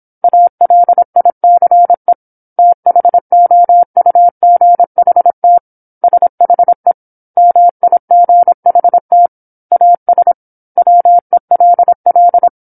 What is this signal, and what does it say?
Morse code, heard as a keyed tone, about 27 wpm, 700 Hz. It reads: ALICE THOUGHT SHE MIGHT AS WELL
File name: 08-Chapter-08 26 WPM